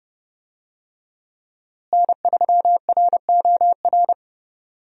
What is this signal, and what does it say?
Morse code, heard as a keyed tone, about 30 wpm, 700 Hz. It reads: N3ROR
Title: USA Call Signs - 5 Characters - Repeated 2x - 30wpm